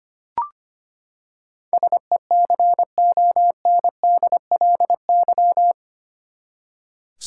Morse code, heard as a keyed tone, about 25 wpm, 700 Hz. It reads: SECONDLY
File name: Eight Letter Words - 25wpm